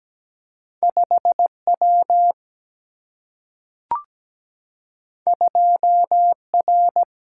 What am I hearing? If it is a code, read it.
Morse code, heard as a keyed tone, about 17 wpm, 700 Hz. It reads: 5W 2R
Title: Single-Letter Single-Number 17wpm